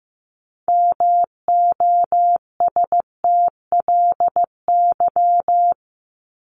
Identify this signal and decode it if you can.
Morse code, heard as a keyed tone, about 15 wpm, 700 Hz. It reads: MOSTLY
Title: Top 1001-1250 Words 15wpm